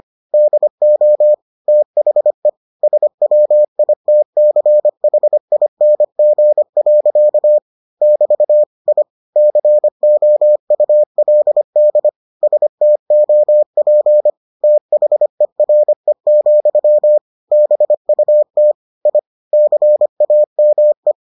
Morse code, heard as a keyed tone, about 25 wpm, 600 Hz. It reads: DO THE SWITCHING. <BT> I COULD STOP THERE, BUT I CAME